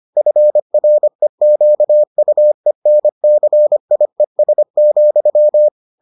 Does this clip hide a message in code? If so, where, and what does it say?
Yes: Morse code from 0.2 to 5.7 s, heard as a keyed tone, about 25 wpm, 600 Hz. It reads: FREQUENCIES,